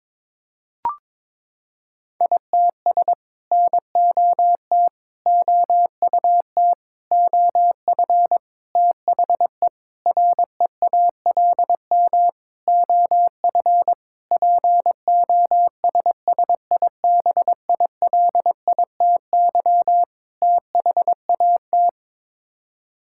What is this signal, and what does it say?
Morse code, heard as a keyed tone, about 21 wpm, 700 Hz. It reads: ITS NOT OUT OF THE REALM OF POSSIBILITY THAT
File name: Sets of 9 Words - 22wpm